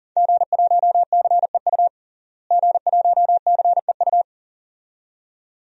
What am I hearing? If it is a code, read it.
Morse code, heard as a keyed tone, about 38 wpm, 700 Hz. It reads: G1CEU G1CEU
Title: DXCC-100 Call Signs - 5 Characters - Repeated 2x - 40wpm